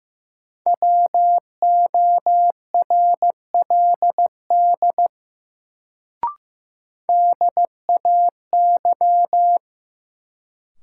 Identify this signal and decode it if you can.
Morse code, heard as a keyed tone, about 15 wpm, 700 Hz. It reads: WORLD DAY
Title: Sentences from Top 200 Words - New Word Drill - 15wpm